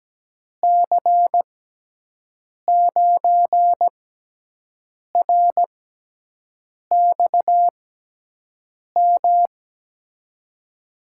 Morse code, heard as a keyed tone, about 17 wpm, spaced as for 6 wpm, 700 Hz. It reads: C9RXM
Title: USA Call Signs - 6 Characters - 6x Character Spacing - 17wpm